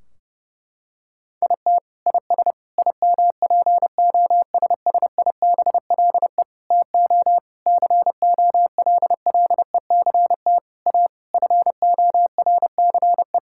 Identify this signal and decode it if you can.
Morse code, heard as a keyed tone, about 30 wpm, 700 Hz. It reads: IT IS IMPOSSIBLE TO COLLECT A FORCE
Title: Sentences from Top 1600 Words 30wpm